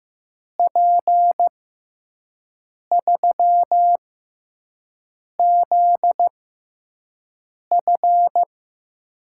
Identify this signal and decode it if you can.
Morse code, heard as a keyed tone, about 15 wpm, spaced as for 5 wpm, 700 Hz. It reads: P3ZF
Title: USA Call Signs - 5 Characters - 6x Character Spacing - 15wpm